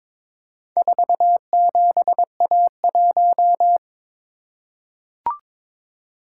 Morse code, heard as a keyed tone, about 22 wpm, 700 Hz. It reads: 47A1
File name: practice / 22wpm / Four Letter-Number Combinations - 22wpm